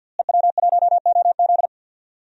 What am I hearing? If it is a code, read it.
Morse code, heard as a keyed tone, about 47 wpm, 700 Hz. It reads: EW1OZ
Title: DXCC-100 Call Signs - 5 Characters - Repeated 3x - 50wpm